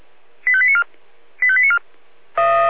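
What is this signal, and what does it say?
Five-tone selective call (ZVEI), heard as quick runs of five tones, tones of about 70 ms. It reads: 85894 85894
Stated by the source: ZVEI five-tone call received on SDR.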